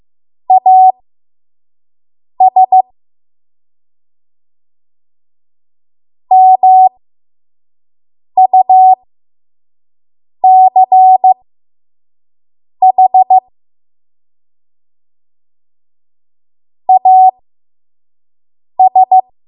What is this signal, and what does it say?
Morse code, heard as a keyed tone, about 15 wpm, spaced as for 5 wpm, 750 Hz. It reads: AS MUCH AS
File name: Morse / 05WPM